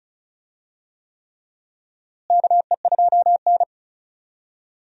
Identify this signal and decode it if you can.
Morse code, heard as a keyed tone, about 36 wpm, 700 Hz. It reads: KE2N
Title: USA Call Signs - 4 Characters - Repeated 3x - 35wpm